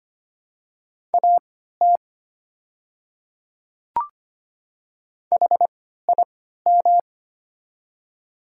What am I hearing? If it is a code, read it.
Morse code, heard as a keyed tone, about 25 wpm, spaced as for 14 wpm, 700 Hz. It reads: AT HIM
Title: 3x Character Spacing - Top 100 Words - 25wpm